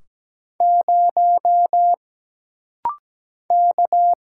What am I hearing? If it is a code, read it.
Morse code, heard as a keyed tone, about 17 wpm, 700 Hz. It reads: 0 K
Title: Single Letter-Number - Rapid-Fire 17wpm